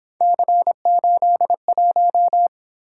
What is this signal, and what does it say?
Morse code, heard as a keyed tone, about 26 wpm, 700 Hz. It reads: C81
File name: Speed-Racing - Three Letter-Number Combinations - 22wpm